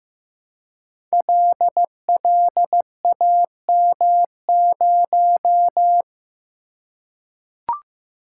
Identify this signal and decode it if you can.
Morse code, heard as a keyed tone, about 15 wpm, 700 Hz. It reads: LLAM0
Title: Code Groups - 5 Letter:Numbers - 15wpm